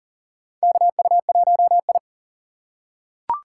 Morse code, heard as a keyed tone, about 40 wpm, 700 Hz. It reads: KU1I
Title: USA Call Signs - 4 Characters - 4x Character Spacing - 40wpm